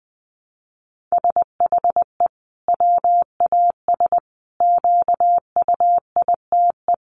Morse code, heard as a keyed tone, about 20 wpm, 700 Hz. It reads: SHE WAS QUITE
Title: Sentences from Top 700 Words 20wpm